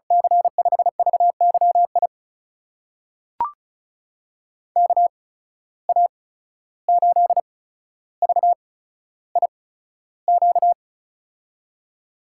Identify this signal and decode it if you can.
Morse code, heard as a keyed tone, about 36 wpm, spaced as for 14 wpm, 700 Hz. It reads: C5VYI KA8VIQ
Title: USA Call Signs - 6 Characters - 8x Character Spacing - 35wpm